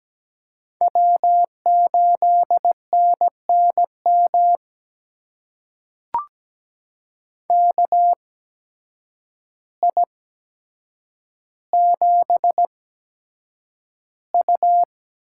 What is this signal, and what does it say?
Morse code, heard as a keyed tone, about 17 wpm, spaced as for 8 wpm, 700 Hz. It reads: W8NNM KI7U
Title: USA Call Signs - 5 Characters - 8x Character Spacing - 17wpm